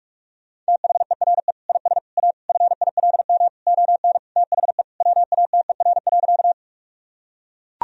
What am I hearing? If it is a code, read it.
Morse code, heard as a keyed tone, about 42 wpm, 700 Hz. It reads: THERE IS A FILM ON THE WATER.
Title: Sentences from Top 1000 Words - Review Part 3 - 45wpm